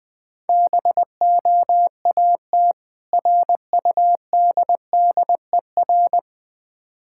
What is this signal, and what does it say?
Morse code, heard as a keyed tone, about 20 wpm, 700 Hz. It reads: BOAT RUDDER
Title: Difficult Words - Speed-Racing with Hints - 20wpm